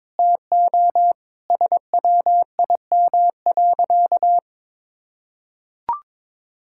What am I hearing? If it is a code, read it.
Morse code, heard as a keyed tone, about 22 wpm, 700 Hz. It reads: TO SWIM.
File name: practice / 22wpm / Sentences with 4-Word Phrases - 22wpm